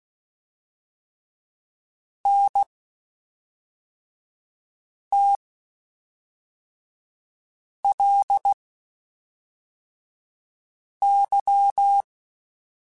Morse code, heard as a keyed tone, about 16 wpm, spaced as for 3 wpm, 800 Hz. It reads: NTLY
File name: Spaced Characters / Spaced Characters-mp3s / Weather5 22 sp @5